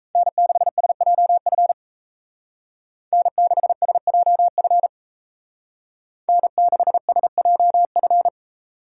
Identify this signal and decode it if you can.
Morse code, heard as a keyed tone, about 34 wpm, 700 Hz. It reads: N6SJF N6SJF N6SJF
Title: Speed Racing - USA 1x3 Call Signs 28wpm